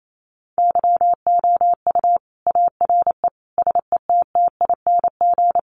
Morse code, heard as a keyed tone, about 28 wpm, 700 Hz. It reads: YOU ARE SETTING
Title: Sentences from Top 1300 Words 28wpm